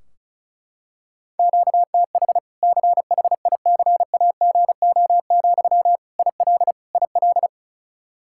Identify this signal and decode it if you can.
Morse code, heard as a keyed tone, about 34 wpm, 700 Hz. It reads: QTH CHICAGO, IL IL
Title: Practice QSO - Exchange 1 - Beginning - 35wpm